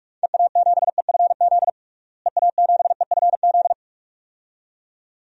Morse code, heard as a keyed tone, about 46 wpm, 700 Hz. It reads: EA7EFZ EA7EFZ
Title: DXCC-100 Call Signs - 6 Characters - Repeated 2x - 45wpm